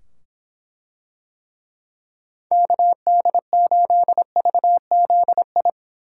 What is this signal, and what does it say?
Morse code, heard as a keyed tone, about 23 wpm, 700 Hz. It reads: KD8VZI